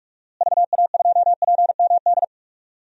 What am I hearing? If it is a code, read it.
Morse code, heard as a keyed tone, about 46 wpm, 700 Hz. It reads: VA2PMD
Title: DXCC-100 Call Signs - 6 Characters - Repeated 3x - 45wpm